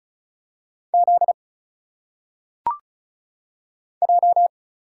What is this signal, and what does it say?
Morse code, heard as a keyed tone, about 36 wpm, 700 Hz. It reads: Z J
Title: Single Letters - Phonetic Pronunciation 35wpm